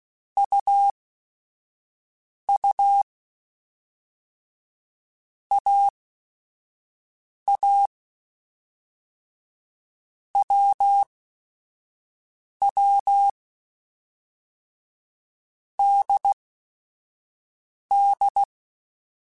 Morse code, heard as a keyed tone, about 16 wpm, spaced as for 5 wpm, 800 Hz. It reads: UU AA WW DD
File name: Basic Sets / Basic Letter Groups Collective7 / AWUNDG@5